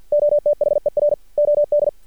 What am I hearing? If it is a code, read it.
Morse code, heard as a keyed tone, about 50 wpm, 590 Hz. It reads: OTHER OD
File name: archived / CRSnet